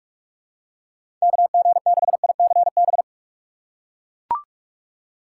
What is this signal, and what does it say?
Morse code, heard as a keyed tone, about 42 wpm, 700 Hz. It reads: KG6ICB
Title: USA Call Signs - 6 Characters - 2x Character Spacing - 45wpm